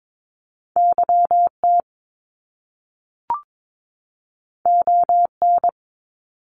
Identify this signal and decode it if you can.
Morse code, heard as a keyed tone, about 22 wpm, 700 Hz. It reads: YT ON
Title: Top Two Letter Combinations in Words 22wpm